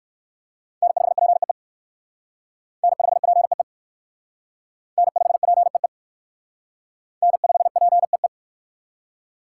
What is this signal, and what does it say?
Morse code, heard as a keyed tone, about 56 wpm, 690 Hz. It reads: N5PEE N5PEE N5PEE N5PEE